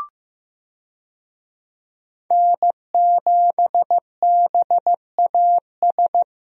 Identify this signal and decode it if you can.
Morse code, heard as a keyed tone, about 15 wpm, 700 Hz. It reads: N7BAS
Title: USA Call Signs - 5 Characters - Repeated 2x - 15wpm